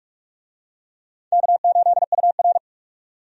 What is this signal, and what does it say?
Morse code, heard as a keyed tone, about 46 wpm, 700 Hz. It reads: K8UR
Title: USA Call Signs - 4 Characters - 45wpm